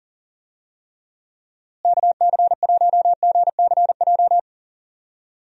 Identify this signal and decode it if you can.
Morse code, heard as a keyed tone, about 38 wpm, 700 Hz. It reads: KC1GCJ
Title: USA Call Signs - 6 Characters - Repeated 3x - 40wpm